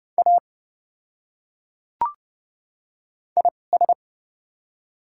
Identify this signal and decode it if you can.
Morse code, heard as a keyed tone, about 30 wpm, spaced as for 22 wpm, 700 Hz. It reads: A IS
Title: 2x Character Spacing - Top 100 Words - 30wpm